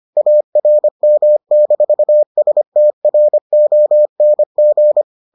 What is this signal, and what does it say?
Morse code, heard as a keyed tone, about 25 wpm, 600 Hz. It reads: ARM-STRONG